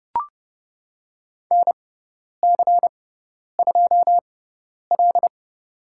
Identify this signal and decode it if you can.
Morse code, heard as a keyed tone, about 28 wpm, spaced as for 10 wpm, 700 Hz. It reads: NC2L
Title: USA Call Signs - 4 Characters - 6x Character Spacing - 30wpm